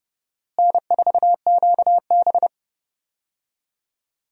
Morse code, heard as a keyed tone, about 30 wpm, 700 Hz. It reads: N4QB
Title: USA Call Signs - 4 Characters - Repeated 2x - 30wpm